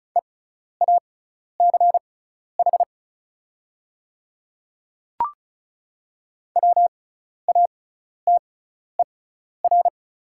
Morse code, heard as a keyed tone, about 36 wpm, spaced as for 12 wpm, 700 Hz. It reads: EACH WATER
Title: 6x Character Spacing - Top 100 Words - 35wpm